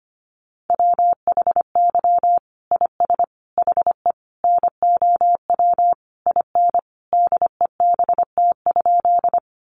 Morse code, heard as a keyed tone, about 25 wpm, 700 Hz. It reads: WHY IS HE NOW IN DEBT?
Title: Sentences from Top 1300 Words 25wpm